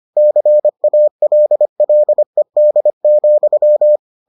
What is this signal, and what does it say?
Morse code, heard as a keyed tone, about 25 wpm, 600 Hz. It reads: CALLED,